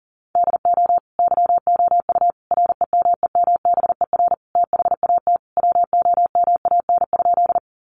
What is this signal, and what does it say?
Morse code, heard as a keyed tone, about 40 wpm, 700 Hz. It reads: DO YOU REMEMBER THAT WOMAN?
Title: Sentences from Top 1100 Words 40wpm